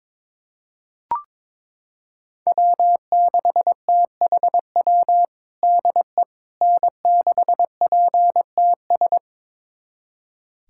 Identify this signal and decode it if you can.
Morse code, heard as a keyed tone, about 22 wpm, 700 Hz. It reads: W6THW DE N6PTS
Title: USA Callsign DE Callsign - 5 Characters - 22wpm